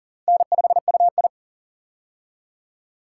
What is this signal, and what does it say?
Morse code, heard as a keyed tone, about 40 wpm, 700 Hz. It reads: N5UI
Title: K1USN SST QSO Practice - 40wpm